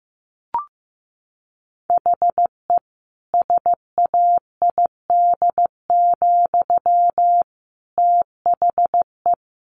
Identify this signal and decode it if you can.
Morse code, heard as a keyed tone, about 15 wpm, 700 Hz. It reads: HE SAID, THE
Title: Sentences from Top 1200 Words 15wpm